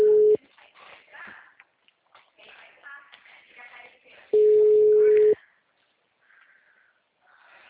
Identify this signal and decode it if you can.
Ringback tone (Europe, 425 Hz).